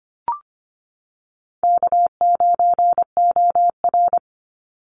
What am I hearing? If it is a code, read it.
Morse code, heard as a keyed tone, about 25 wpm, 700 Hz. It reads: K9OR
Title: Call Signs Repeated 2x - 25wpm